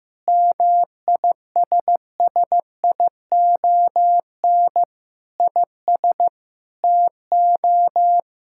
Morse code, heard as a keyed tone, about 15 wpm, 700 Hz. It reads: MISSION IS TO
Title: Sentences from Top 1600 Words 15wpm